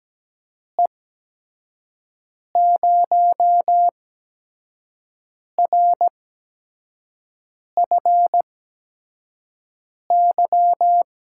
Morse code, heard as a keyed tone, about 17 wpm, spaced as for 5 wpm, 700 Hz. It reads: E0RFY
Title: USA Call Signs - 6 Characters - 8x Character Spacing - 17wpm